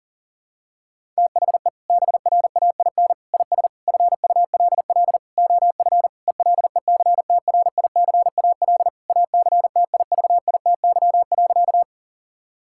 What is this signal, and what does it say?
Morse code, heard as a keyed tone, about 40 wpm, 700 Hz. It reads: THE BRAIN IS FULL OF ELECTRICAL ACTIVITY.
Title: Sentences from Top 1000 Words - Review Part 6 - 40wpm